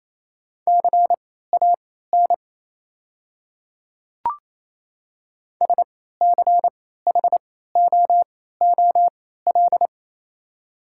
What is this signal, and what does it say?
Morse code, heard as a keyed tone, about 28 wpm, spaced as for 16 wpm, 700 Hz. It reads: CAN SCHOOL
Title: 3x Character Spacing - Top 300 Words - 28wpm